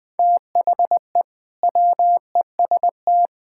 Morse code, heard as a keyed tone, about 21 wpm, 700 Hz. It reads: THE WEST
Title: Build A Phrase - 5-Word Phrases - 20wpm